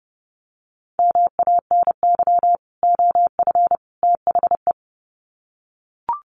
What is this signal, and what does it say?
Morse code, heard as a keyed tone, about 29 wpm, 700 Hz. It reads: MANY OF THE
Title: Sets of 3 Words 30wpm